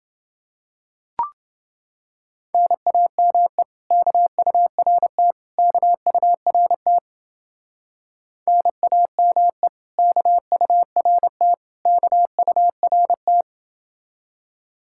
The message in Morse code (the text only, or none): NAME KURT KURT NAME KURT KURT